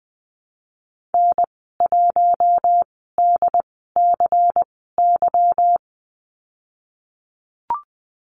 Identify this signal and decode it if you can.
Morse code, heard as a keyed tone, about 20 wpm, spaced as for 15 wpm, 700 Hz. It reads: N1DCY
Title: 154 Lesson 25g - Letter p - Call Signs 20wpm